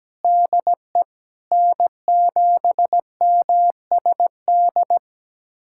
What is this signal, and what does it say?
Morse code, heard as a keyed tone, about 17 wpm, 700 Hz. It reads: DE N7MSD
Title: USA Callsign DE Callsign 17wpm